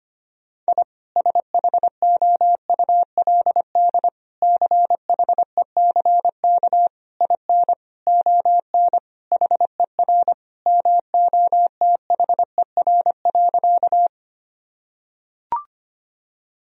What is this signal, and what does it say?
Morse code, heard as a keyed tone, about 25 wpm, 700 Hz. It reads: I SHOULD CHECK IN ON HER MOTHER.
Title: Sentences from Top 1000 Words - Review Part 3 - 25wpm